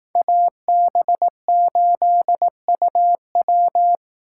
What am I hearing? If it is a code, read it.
Morse code, heard as a keyed tone, about 18 wpm, 700 Hz. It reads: AB8UW
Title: Speed Racing - USA 2x2 Call Signs 15wpm